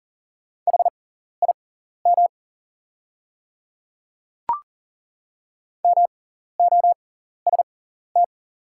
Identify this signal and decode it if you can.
Morse code, heard as a keyed tone, about 40 wpm, spaced as for 14 wpm, 690 Hz. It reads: HIM MOST
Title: 6x Character Spacing - Top 100 Words - 40wpm